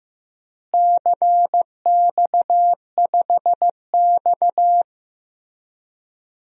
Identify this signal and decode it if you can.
Morse code, heard as a keyed tone, about 15 wpm, 700 Hz. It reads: CX5X
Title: DXCC-100 Call Signs - 4 Characters - Repeated 2x - 15wpm